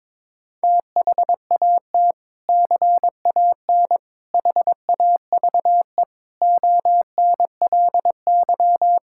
Morse code, heard as a keyed tone, about 22 wpm, 700 Hz. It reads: THAT CAN HAVE ONLY
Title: Sentences from Top 1000 Words - Review Part 6 - 22wpm